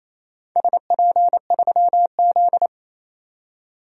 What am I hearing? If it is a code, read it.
Morse code, heard as a keyed tone, about 28 wpm, 700 Hz. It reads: SP3Z
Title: DXCC-100 Call Signs - 4 Characters - 28wpm